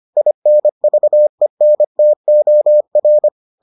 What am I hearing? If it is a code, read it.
Morse code, heard as a keyed tone, about 25 wpm, 600 Hz. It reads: INVENTOR